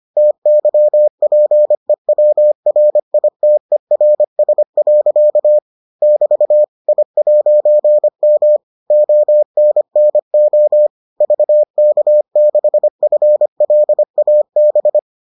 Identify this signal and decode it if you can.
Morse code, heard as a keyed tone, about 25 wpm, 600 Hz. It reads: TYPEWRITERS. <BT> I'M ONNO VK6FLAB